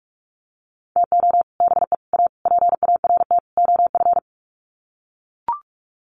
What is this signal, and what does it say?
Morse code, heard as a keyed tone, about 44 wpm, 700 Hz. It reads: TO BE A PART OF
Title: Sets of 5 Words 45wpm